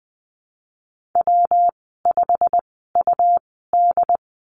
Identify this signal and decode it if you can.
Morse code, heard as a keyed tone, about 20 wpm, spaced as for 15 wpm, 700 Hz. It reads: W5UD
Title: 173 Lesson 28g - Number 9 - Call Signs 20wpm